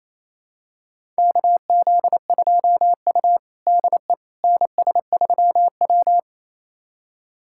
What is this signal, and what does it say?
Morse code, heard as a keyed tone, about 28 wpm, 700 Hz. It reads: KZ2U DE NS3W
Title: USA Callsign DE Callsign - 4 Characters - 28wpm